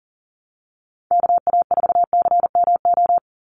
Morse code, heard as a keyed tone, about 40 wpm, 700 Hz. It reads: KA4CMO
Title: USA Call Signs 40wpm